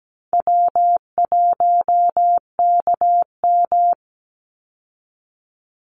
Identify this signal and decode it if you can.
Morse code, heard as a keyed tone, about 17 wpm, 700 Hz. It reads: W1KM
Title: Call Signs Repeated 2x - 17wpm